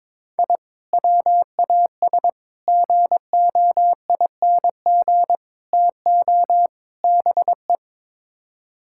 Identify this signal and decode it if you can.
Morse code, heard as a keyed tone, about 22 wpm, 700 Hz. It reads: I WAS GOING TO BE
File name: practice / 22wpm / Build A Phrase - 5-Word Phrases - 22wpm